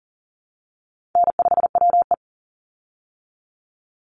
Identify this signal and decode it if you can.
Morse code, heard as a keyed tone, about 40 wpm, 700 Hz. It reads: N5WE
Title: Easy Call Signs Repeated 3x - 40wpm